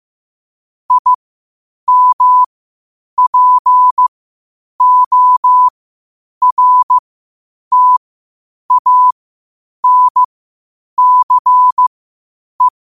Morse code, heard as a keyed tone, about 15 wpm, spaced as for 8 wpm, 1000 Hz. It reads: IMPORTANCE
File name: qotdfiles / 8wpm